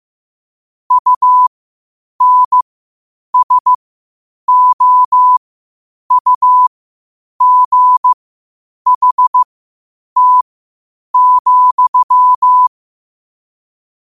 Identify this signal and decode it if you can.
Morse code, heard as a keyed tone, about 15 wpm, spaced as for 8 wpm, 1000 Hz. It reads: UNSOUGHT,
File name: qotdfiles / 8wpm